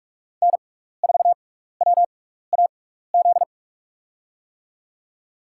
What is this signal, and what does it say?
Morse code, heard as a keyed tone, about 44 wpm, spaced as for 15 wpm, 690 Hz. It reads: N4WAZ
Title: USA Call Signs - 5 Characters - 6x Character Spacing - 45wpm